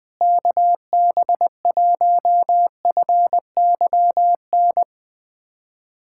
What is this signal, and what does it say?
Morse code, heard as a keyed tone, about 20 wpm, 700 Hz. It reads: KB1FYN
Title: USA Callsign DE Callsign - 6 Characters - 20wpm